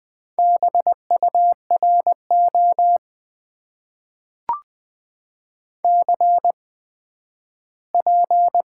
Morse code, heard as a keyed tone, about 20 wpm, spaced as for 12 wpm, 700 Hz. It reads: BURO CP
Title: Top 90 QSO Elements - 8x Character Spacing - 20wpm